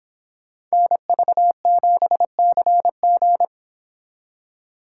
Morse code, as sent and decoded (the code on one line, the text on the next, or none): -. ...- --... -.-. --.
NV7CG